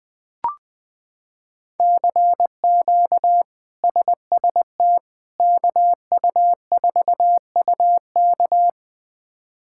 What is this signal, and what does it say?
Morse code, heard as a keyed tone, about 20 wpm, 700 Hz. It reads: CQ SST KU4UK
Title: K1USN SST QSO Practice - 20wpm